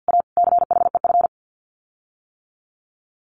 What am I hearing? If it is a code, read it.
Morse code, heard as a keyed tone, about 51 wpm, 700 Hz. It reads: A CHEF
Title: Common QSO Phrases Encore 50wpm